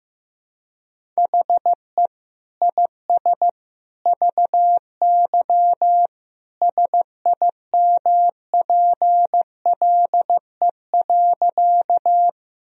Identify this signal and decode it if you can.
Morse code, heard as a keyed tone, about 15 wpm, 700 Hz. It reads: HE IS VY SIMPLE.
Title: Sentences from Top 400 Words and Abbreviations 15wpm